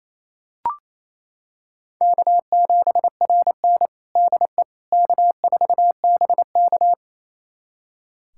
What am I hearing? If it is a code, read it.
Morse code, heard as a keyed tone, about 28 wpm, 700 Hz. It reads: K7RN DE K4BK
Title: USA Callsign DE Callsign - 4 Characters - 28wpm